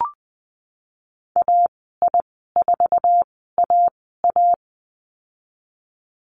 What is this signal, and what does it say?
Morse code, heard as a keyed tone, about 20 wpm, spaced as for 14 wpm, 700 Hz. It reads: AI4AA
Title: 048 Lesson 9g - Number 4 - Call Signs 20wpm